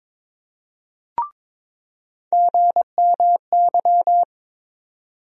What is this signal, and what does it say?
Morse code, heard as a keyed tone, about 22 wpm, 700 Hz. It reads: GMY
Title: USA Call Sign Postfixes - 3 Letters 22wpm